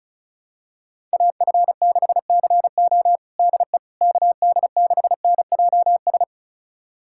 Morse code, heard as a keyed tone, about 36 wpm, 700 Hz. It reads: AF6CO DE KD6NJS
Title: USA Callsign DE Callsign 35wpm